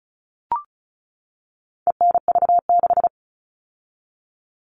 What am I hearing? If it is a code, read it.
Morse code, heard as a keyed tone, about 34 wpm, 700 Hz. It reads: ENV6
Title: All Cut Numbers - 4-Digit 35wpm